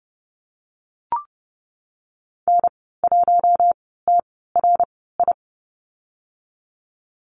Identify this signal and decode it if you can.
Morse code, heard as a keyed tone, about 32 wpm, spaced as for 17 wpm, 690 Hz. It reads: N1TRI
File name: Lesson 15g - Number 5 - Call Signs 30wpm